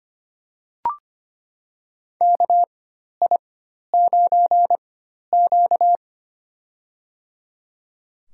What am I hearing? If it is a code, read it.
Morse code, heard as a keyed tone, about 25 wpm, spaced as for 12 wpm, 700 Hz. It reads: KI9Q
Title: USA Call Signs - 4 Characters - 4x Character Spacing - 25wpm